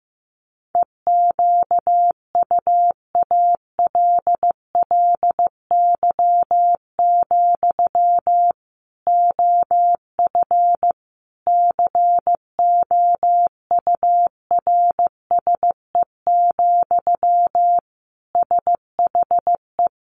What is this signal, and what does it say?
Morse code, heard as a keyed tone, about 15 wpm, 700 Hz. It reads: EQUALLY, OF COURSE, SHE
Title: Sentences from Top 1400 Words 15wpm